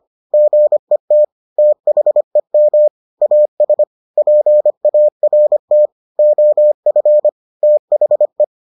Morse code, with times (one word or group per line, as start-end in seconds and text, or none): 0.3-1.2 GET
1.6-2.9 THEM
3.2-3.8 AS
4.2-5.9 PART
6.2-7.3 OF
7.6-8.4 THE